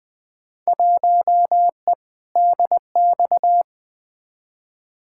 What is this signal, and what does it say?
Morse code, heard as a keyed tone, about 20 wpm, 700 Hz. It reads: 1E DX
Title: Field Day Exchanges Encore - 20wpm